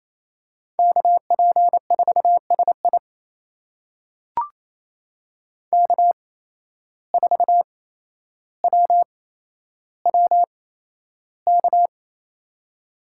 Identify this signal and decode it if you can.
Morse code, heard as a keyed tone, about 28 wpm, spaced as for 12 wpm, 700 Hz. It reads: KP4SI K4WWK
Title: USA Call Signs - 8x Character Spacing - 28wpm